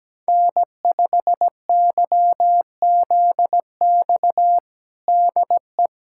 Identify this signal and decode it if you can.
Morse code, heard as a keyed tone, about 17 wpm, 700 Hz. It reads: N5YZX DE
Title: USA Callsign DE Callsign - 5 Characters - 17wpm